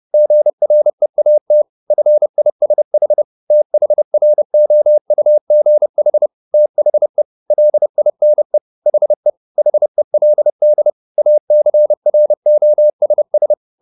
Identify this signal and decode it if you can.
Morse code, heard as a keyed tone, about 30 wpm, 600 Hz. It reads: GREAT FISH THROUGH THE LINE HE HELD ACROSS
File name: the old man and the sea / the old man and the sea 1 30wpm